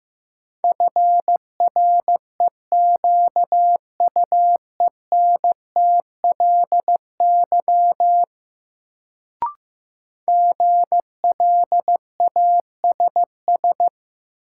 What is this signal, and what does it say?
Morse code, heard as a keyed tone, about 15 wpm, 700 Hz. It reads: FREQUENTLY GLASS
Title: Sentences from Top 1300 Words - New Word Drill Alt - 15wpm